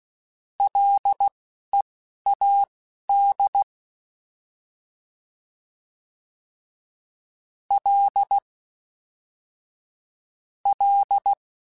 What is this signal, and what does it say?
Morse code, heard as a keyed tone, about 16 wpm, spaced as for 11 wpm, 800 Hz. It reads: LEAD L L